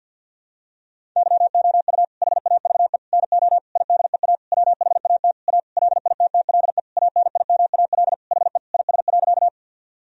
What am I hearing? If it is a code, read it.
Morse code, heard as a keyed tone, about 53 wpm, 700 Hz. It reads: YOU HAVE NO IDEA WHAT A LITTLE ANIMAL HE IS.
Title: Sentences from Top 1000 Words - Review Part 1 - 50wpm